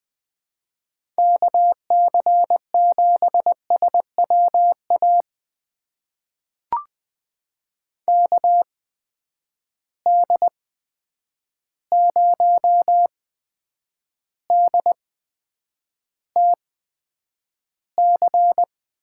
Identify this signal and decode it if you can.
Morse code, heard as a keyed tone, about 20 wpm, spaced as for 9 wpm, 700 Hz. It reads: KC7SWA KD0DTC
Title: USA Call Signs - 6 Characters - 8x Character Spacing - 20wpm